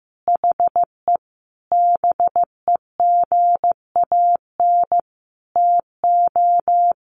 Morse code, heard as a keyed tone, about 15 wpm, 700 Hz. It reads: HE BEGAN TO
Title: Sentences from Top 1500 Words 15wpm